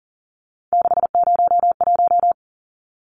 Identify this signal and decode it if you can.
Morse code, heard as a keyed tone, about 40 wpm, 700 Hz. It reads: 601
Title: Numbers - 3-Digit 40wpm